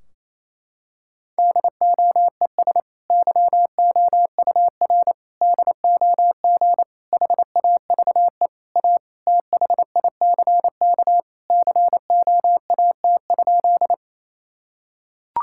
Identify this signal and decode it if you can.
Morse code, heard as a keyed tone, about 28 wpm, 700 Hz. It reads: DOES YOUR DOG HAVE A THICK COAT?
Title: Sentences from Top 1600 Words 28wpm